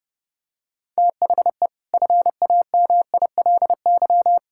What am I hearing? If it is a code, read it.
Morse code, heard as a keyed tone, about 30 wpm, 700 Hz. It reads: THE FAMILY
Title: Sentences from Top 1000 Words - Review Part 6 - 30wpm